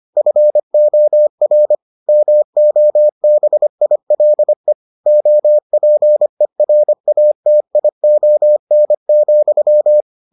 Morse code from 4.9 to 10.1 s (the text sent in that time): OPERATION,